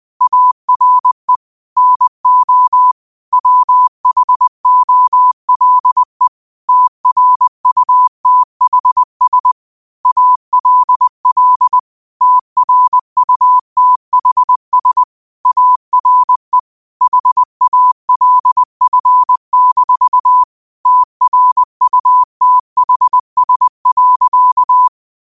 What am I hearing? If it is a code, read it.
Morse code, heard as a keyed tone, about 20 wpm, 1000 Hz. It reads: ARE NO WHOLE TRUTHS ALL TRUTHS ARE HALF-TRUTHS.
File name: qotdfiles / 20wpm